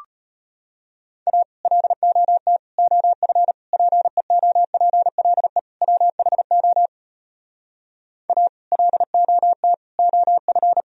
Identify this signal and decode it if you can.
Morse code, heard as a keyed tone, about 33 wpm, 700 Hz. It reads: A LOT OF PEOPLE WHO A LOT OF